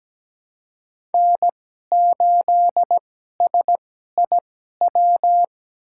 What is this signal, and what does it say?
Morse code, heard as a keyed tone, about 17 wpm, spaced as for 12 wpm, 700 Hz. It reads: N8SIW
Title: USA Call Signs - 5 Characters - 2x Character Spacing - 17wpm